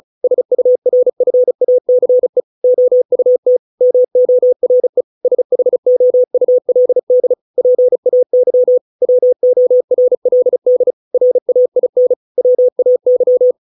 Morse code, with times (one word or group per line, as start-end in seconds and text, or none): 0.2-2.4 SURFACE
2.6-3.6 OUT
3.8-5.0 MORE
5.2-7.3 SHOULD
7.6-8.8 PAY
9.0-10.9 WORLD
11.1-12.1 RAIN
12.4-13.5 WAY